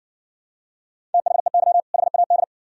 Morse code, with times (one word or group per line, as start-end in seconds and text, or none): 1.1-1.8 THEY
1.9-2.4 HAD